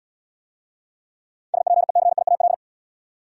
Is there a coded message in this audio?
Yes: Morse code from 1.5 to 2.5 s, heard as a keyed tone, about 77 wpm, 700 Hz. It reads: SPECIAL